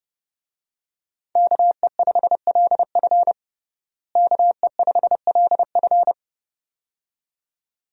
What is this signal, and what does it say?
Morse code, heard as a keyed tone, about 30 wpm, 700 Hz. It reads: KE5LF KE5LF
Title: USA Call Signs - 5 Characters - Repeated 2x - 30wpm